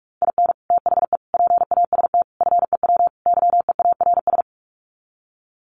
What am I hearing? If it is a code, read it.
Morse code, heard as a keyed tone, about 44 wpm, 700 Hz. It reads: IN THE PAST FEW YEARS
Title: Sets of 5 Words 45wpm